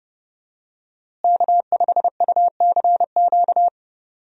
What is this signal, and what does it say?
Morse code, heard as a keyed tone, about 32 wpm, 700 Hz. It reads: K5UCQ
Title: USA Call Signs - 5 Characters - Repeated 3x - 30wpm